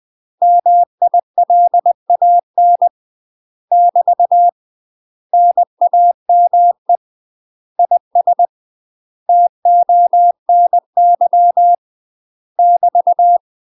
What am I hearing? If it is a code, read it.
Morse code, heard as a keyed tone, about 20 wpm, 700 Hz. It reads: MILAN <BT> NAME IS TONY <BT>
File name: UK MASTER QSOs TRAINING / 20WPM / UK MASTER QSOs 0011